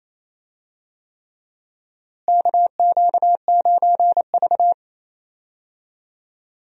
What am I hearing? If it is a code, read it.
Morse code, heard as a keyed tone, about 28 wpm, 700 Hz. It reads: KQ9V
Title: USA Call Signs - 4 Characters - Repeated 3x - 28wpm